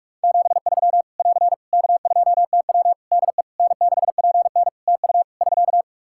Morse code, heard as a keyed tone, about 46 wpm, 700 Hz. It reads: 73 <AR> K2TW DE N6PN TU <SK>
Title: Practice QSO - Exchange 4 - Ending - 45wpm